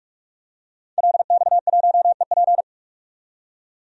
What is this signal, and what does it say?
Morse code, heard as a keyed tone, about 42 wpm, 700 Hz. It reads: LX1EP
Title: DXCC-100 Call Signs - 5 Characters - 45wpm